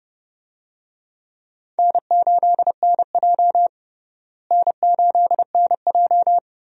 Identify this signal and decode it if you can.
Morse code, heard as a keyed tone, about 30 wpm, 700 Hz. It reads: N8NJ N8NJ
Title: USA Call Signs - 4 Characters - Repeated 2x - 30wpm